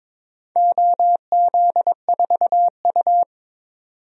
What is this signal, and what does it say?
Morse code, heard as a keyed tone, about 22 wpm, 700 Hz. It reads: OZ4U
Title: DXCC-100 Call Signs - 4 Characters - 22wpm